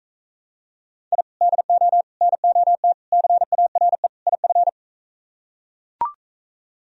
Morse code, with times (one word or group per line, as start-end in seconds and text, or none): 1.1-1.2 I
1.4-2.0 DO
2.2-2.9 NOT
3.1-4.1 CARE
4.3-4.7 IF